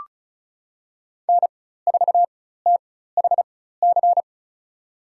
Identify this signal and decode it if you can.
Morse code, heard as a keyed tone, about 36 wpm, spaced as for 16 wpm, 700 Hz. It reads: N4THC
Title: USA Call Signs - 5 Characters - 4x Character Spacing - 35wpm